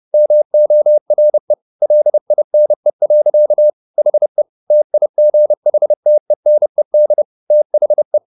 Morse code, heard as a keyed tone, about 30 wpm, 600 Hz. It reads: MORE LINE. HE TIGHTENED THE
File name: the old man and the sea / the old man and the sea 1 30wpm